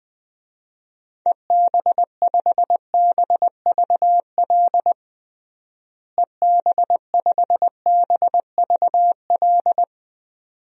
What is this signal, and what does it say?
Morse code, heard as a keyed tone, about 21 wpm, 700 Hz. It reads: EB5BVL EB5BVL
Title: DXCC-100 Call Signs - 6 Characters - Repeated 2x - 20wpm